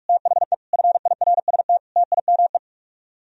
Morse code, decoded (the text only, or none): THE FIRST TIME